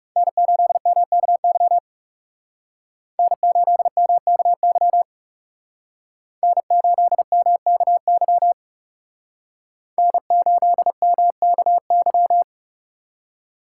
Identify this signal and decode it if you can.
Morse code, heard as a keyed tone, about 36 wpm, 700 Hz. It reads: N8MKY N8MKY N8MKY N8MKY
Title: Speed Racing - USA 1x3 Call Signs 30wpm